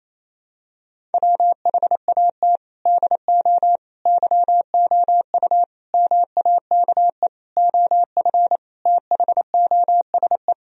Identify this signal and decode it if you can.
Morse code, heard as a keyed tone, about 28 wpm, 700 Hz. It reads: WHAT DO YOU MAKE OF THOSE
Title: Sentences from Top 1000 Words - Review Part 1 - 28wpm